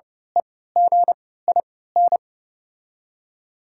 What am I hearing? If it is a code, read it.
Morse code, heard as a keyed tone, about 29 wpm, spaced as for 17 wpm, 700 Hz. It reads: EGIN
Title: 3x Character Spacing - Top 500 Words - 30wpm